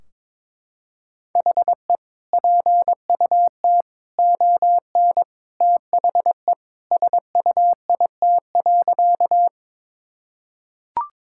Morse code, heard as a keyed tone, about 22 wpm, 700 Hz. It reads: HE PUT ON THE SUIT.
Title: Sentences from Top 1600 Words 22wpm